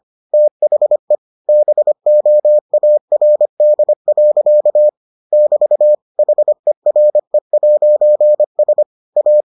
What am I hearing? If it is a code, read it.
Morse code, heard as a keyed tone, about 25 wpm, 600 Hz. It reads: THE BOARD. <BT> HERE'S A